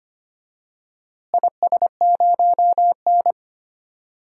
Morse code, heard as a keyed tone, about 25 wpm, 700 Hz. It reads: IS0N
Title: DXCC-100 Call Signs - 4 Characters - Repeated 2x - 25wpm